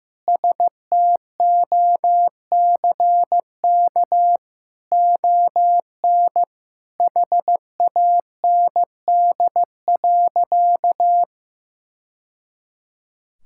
Morse code, heard as a keyed tone, about 15 wpm, 700 Hz. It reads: STOCK ON HAND.
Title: Sentences from Top 1000 Words - Review Part 5 - 15wpm